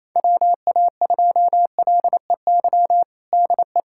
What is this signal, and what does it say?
Morse code, heard as a keyed tone, about 28 wpm, 700 Hz. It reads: WA2LEY DE
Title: USA Callsign DE Callsign 28wpm